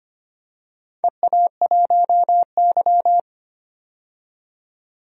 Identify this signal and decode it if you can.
Morse code, heard as a keyed tone, about 25 wpm, 700 Hz. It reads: EA1Y
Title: DXCC-100 Call Signs - 4 Characters - 25wpm